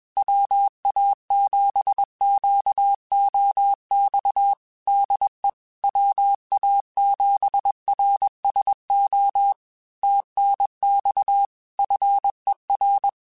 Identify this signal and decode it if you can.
Morse code, heard as a keyed tone, about 21 wpm, 800 Hz. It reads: WA7QOX DE WA7RSO TNX FER
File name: QSOs / QSOs-mp3s / WA7QOX 2@20